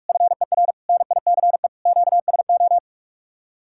Morse code, heard as a keyed tone, about 44 wpm, 700 Hz. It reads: FER NICE QSO